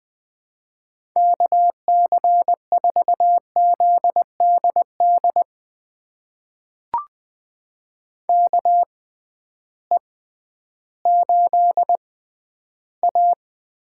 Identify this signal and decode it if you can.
Morse code, heard as a keyed tone, about 20 wpm, spaced as for 12 wpm, 700 Hz. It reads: KC4ZDD KE8A
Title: USA Call Signs - 6 Characters - 6x Character Spacing - 20wpm